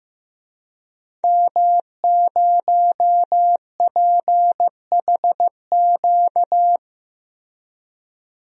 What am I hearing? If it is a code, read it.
Morse code, heard as a keyed tone, about 15 wpm, 700 Hz. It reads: M0PHQ
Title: DXCC-100 Call Signs - 5 Characters - Repeated 3x - 15wpm